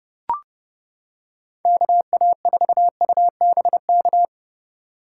Morse code, heard as a keyed tone, about 29 wpm, 700 Hz. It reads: KA4UBK